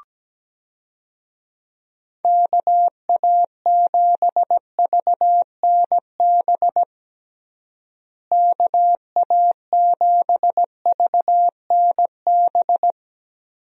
Morse code, heard as a keyed tone, about 17 wpm, 700 Hz. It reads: KA7VNB KA7VNB
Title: USA Call Signs - 6 Characters - Repeated 3x - 17wpm